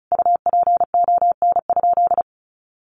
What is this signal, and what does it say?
Morse code, heard as a keyed tone, about 34 wpm, 700 Hz. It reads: UPON?
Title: Sentences from Top 1100 Words 35wpm